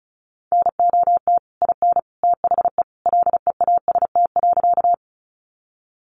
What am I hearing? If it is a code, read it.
Morse code, heard as a keyed tone, about 34 wpm, 700 Hz. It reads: NOT IN THE LEAST.
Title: Sentences from Top 700 Words 35wpm